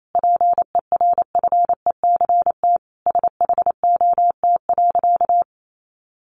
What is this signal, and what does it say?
Morse code, heard as a keyed tone, about 28 wpm, 700 Hz. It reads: PERFECT SHOT.
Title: Sentences from Top 1100 Words 28wpm